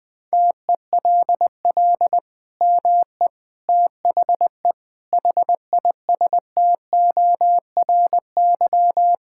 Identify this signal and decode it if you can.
Morse code, heard as a keyed tone, about 21 wpm, 700 Hz. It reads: TELL ME THE HISTORY
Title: Sentences from Top 1000 Words - Review Part 5 - 20wpm